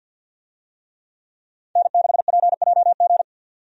Morse code, heard as a keyed tone, about 50 wpm, 700 Hz. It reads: N6PJG
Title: USA Call Signs - 5 Characters - 50wpm